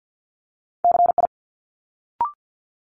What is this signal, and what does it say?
Morse code, heard as a keyed tone, about 50 wpm, 700 Hz. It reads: CI
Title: Multi-letter Phonemes Quick Fire 50wpm